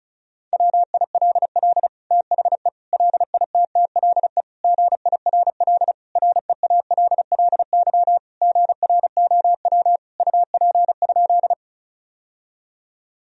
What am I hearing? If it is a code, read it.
Morse code, heard as a keyed tone, about 36 wpm, 700 Hz. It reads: WILL THE LITTLE GIRL REALLY GROW UP?
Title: Sentences from Top 1000 Words - Review Part 1 - 35wpm